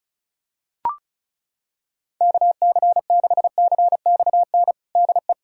Morse code, heard as a keyed tone, about 34 wpm, 700 Hz. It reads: KC6CXN DE
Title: USA Callsign DE Callsign - 6 Characters - 35wpm